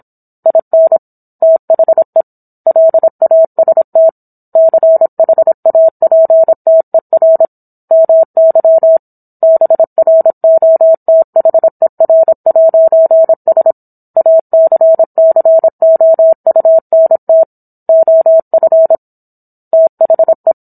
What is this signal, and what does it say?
Morse code, heard as a keyed tone, about 27 wpm, 650 Hz. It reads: IN THE LAST CHAPTER MY BROTHER'S ACCOUNT OF THE
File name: Book One / 17-Chapter 17 - 26 WPM